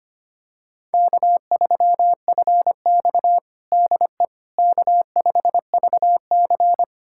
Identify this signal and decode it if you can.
Morse code, heard as a keyed tone, about 25 wpm, 700 Hz. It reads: K3FX DE K5VC
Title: USA Callsign DE Callsign - 4 Characters - 25wpm